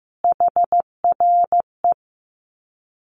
Morse code, heard as a keyed tone, about 15 wpm, 700 Hz. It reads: HRE
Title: Common QSO Phrases 15wpm